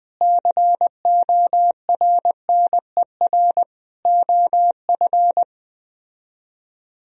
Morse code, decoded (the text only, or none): CORNER OF